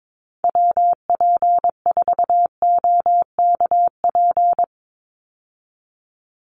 Morse code, heard as a keyed tone, about 22 wpm, 700 Hz. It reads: WP4OKP
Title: USA Call Signs 22wpm